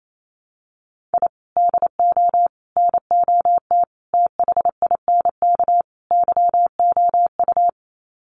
Morse code, heard as a keyed tone, about 28 wpm, 700 Hz. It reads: I DO NOT THINK YOU
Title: Sets of 5 Words 28wpm